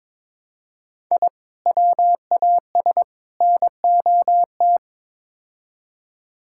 Morse code, heard as a keyed tone, about 21 wpm, 700 Hz. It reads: I WAS NOT
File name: practice / 22wpm / Build A Phrase - 5-Word Phrases - 22wpm